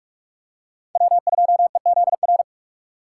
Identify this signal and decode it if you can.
Morse code, heard as a keyed tone, about 46 wpm, 700 Hz. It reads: W2EZR
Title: USA Call Signs - 5 Characters - 4x Character Spacing - 45wpm